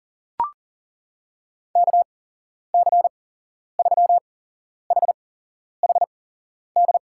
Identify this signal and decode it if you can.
Morse code, heard as a keyed tone, about 40 wpm, spaced as for 11 wpm, 700 Hz. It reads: KC3HHD
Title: USA Call Signs - 6 Characters - 8x Character Spacing - 40wpm